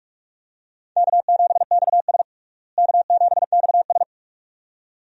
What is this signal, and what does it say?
Morse code, heard as a keyed tone, about 45 wpm, 700 Hz. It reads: K7XS K7XS
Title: USA Call Signs - 4 Characters - Repeated 2x - 45wpm